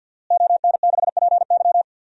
Morse code, heard as a keyed tone, about 50 wpm, 700 Hz. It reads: ON6PY
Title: DXCC-100 Call Signs - 5 Characters - Repeated 2x - 50wpm